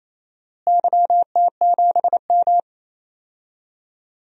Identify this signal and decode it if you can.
Morse code, heard as a keyed tone, about 28 wpm, 700 Hz. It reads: YT7M
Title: DXCC-100 Call Signs - 4 Characters - Repeated 3x - 28wpm